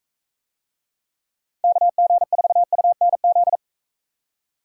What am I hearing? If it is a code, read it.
Morse code, heard as a keyed tone, about 40 wpm, 700 Hz. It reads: KG4UNZ